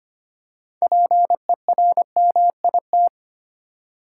Morse code, heard as a keyed tone, about 25 wpm, 700 Hz. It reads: PERMIT
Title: Six Letter Words - 25wpm